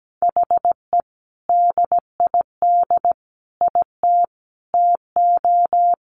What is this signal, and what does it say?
Morse code, heard as a keyed tone, about 17 wpm, 700 Hz. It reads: HE DID IT TO
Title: Sentences from Top 1500 Words 17wpm